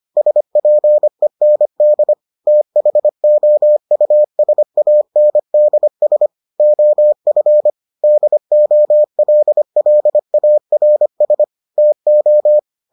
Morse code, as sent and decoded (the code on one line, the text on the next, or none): ... .--. -. -.. / - .... --- ..- ... .- -. -.. ... / --- ..-. / -.. --- .-.. .-.. .- .-. ... / - ---
SPEND THOUSANDS OF DOLLARS TO